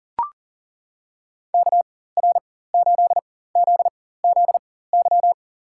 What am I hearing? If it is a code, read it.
Morse code, heard as a keyed tone, about 40 wpm, spaced as for 19 wpm, 700 Hz. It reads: KR8ZZY
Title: USA Call Signs - 4x Character Spacing - 40wpm